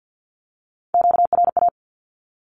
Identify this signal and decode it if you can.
Morse code, heard as a keyed tone, about 50 wpm, 700 Hz. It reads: QRA
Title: Q-Codes with Spoken Meaning - 50wpm